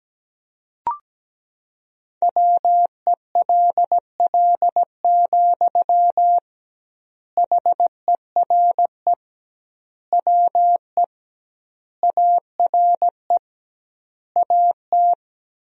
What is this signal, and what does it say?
Morse code, heard as a keyed tone, about 17 wpm, 700 Hz. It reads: WELL, HERE WE ARE AT
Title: 2x Word-Spacing - Sentences from Top 300 Words - 17wpm